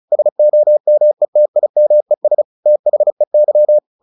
Morse code, heard as a keyed tone, about 34 wpm, 600 Hz. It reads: SOMETIMES THEY
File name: the old man and the sea / the old man and the sea 1 35wpm